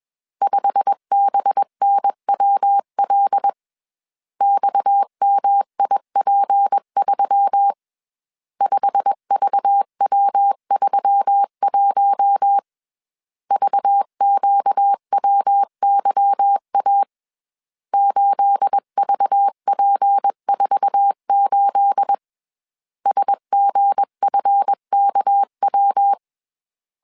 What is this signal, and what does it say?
Morse code, heard as a keyed tone, about 21 wpm, 800 Hz. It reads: 5BNWL XMIP3 5VW31 VQWYA 8VP48 SGFKW
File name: CW / DET / 1 9 1500C S C 20wpm 2310